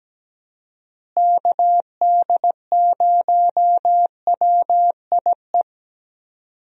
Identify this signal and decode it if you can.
Morse code, heard as a keyed tone, about 17 wpm, 700 Hz. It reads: KD0WIE
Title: USA Call Signs - 6 Characters - Repeated 2x - 17wpm